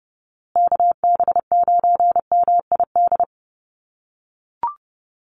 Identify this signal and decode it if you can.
Morse code, heard as a keyed tone, about 30 wpm, 700 Hz. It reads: KB9MID
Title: USA Call Signs Repeated 2x 30wpm